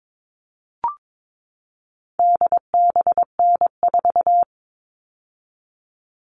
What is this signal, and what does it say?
Morse code, heard as a keyed tone, about 22 wpm, 700 Hz. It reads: DBN4
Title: All Cut Numbers - 4-Digit 22wpm